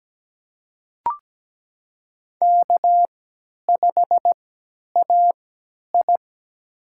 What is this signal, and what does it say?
Morse code, heard as a keyed tone, about 17 wpm, spaced as for 10 wpm, 700 Hz. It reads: K5AI
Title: USA Call Signs - 4 Characters - 3x Character Spacing - 17wpm